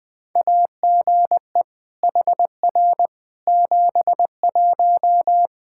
Morse code, heard as a keyed tone, about 20 wpm, 700 Hz. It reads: AGE HR 71
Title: Practice QSO - Exchange 3 - More Info - 20wpm